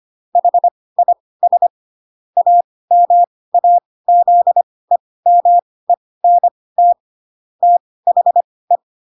Morse code, heard as a keyed tone, about 25 wpm, spaced as for 18 wpm, 700 Hz. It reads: HIS AMAZEMENT THE